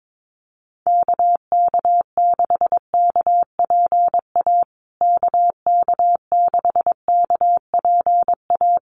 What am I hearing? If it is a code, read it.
Morse code, heard as a keyed tone, about 22 wpm, 700 Hz. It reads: KK6KPA KK6KPA
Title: USA Call Signs Repeated 3x 22wpm